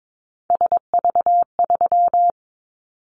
Morse code, heard as a keyed tone, about 22 wpm, 700 Hz. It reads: SV3
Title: Callsign Prefixes for Top 50 DXCC - 22wpm